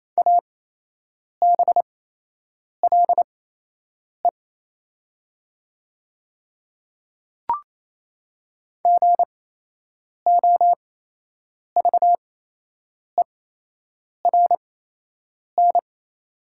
Morse code, heard as a keyed tone, about 27 wpm, spaced as for 8 wpm, 700 Hz. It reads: ABLE GOVERN
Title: 8x Character Spacing - Top 500 Words - 28wpm